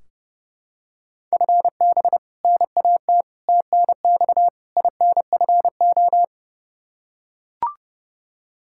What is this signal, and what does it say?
Morse code, heard as a keyed tone, about 30 wpm, 700 Hz. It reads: FB NAT TNX INFO
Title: Practice QSO - Exchange 3 - More Info - 30wpm